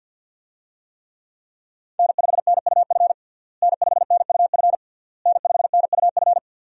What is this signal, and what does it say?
Morse code, heard as a keyed tone, about 49 wpm, 700 Hz. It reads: N5NUF N5NUF N5NUF
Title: USA Call Signs - 5 Characters - Repeated 3x - 50wpm